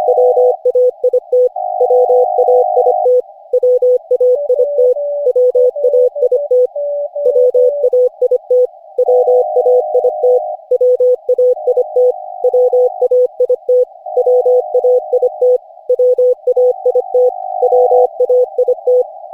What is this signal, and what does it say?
Morse code, heard as a keyed tone, about 25 wpm, 500 Hz. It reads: WAIT WAIT WAIT WAIT WAIT WAIT WAIT WAIT WAIT WAIT WAIT
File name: cw / Session5 cw training